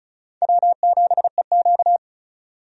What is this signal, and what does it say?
Morse code, heard as a keyed tone, about 36 wpm, 700 Hz. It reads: W7EQ
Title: USA Call Signs - 4 Characters - 35wpm